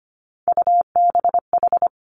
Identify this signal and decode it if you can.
Morse code, heard as a keyed tone, about 25 wpm, 700 Hz. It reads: UBH
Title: Three Challenging Letter-Number Combinations 25wpm